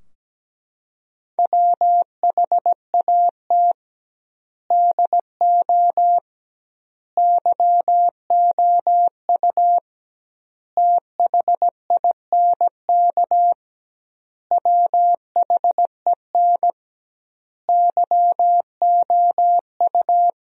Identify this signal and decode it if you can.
Morse code, heard as a keyed tone, about 17 wpm, 700 Hz. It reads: WHAT DO YOU THINK WHEN YOU
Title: 2x Word-Spacing - Sentences from Top 300 Words - 17wpm